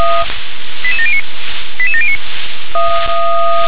Five-tone selective call (ZVEI), heard as quick runs of five tones, tones of about 70 ms. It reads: 88709 88709
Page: SDR/ZVEI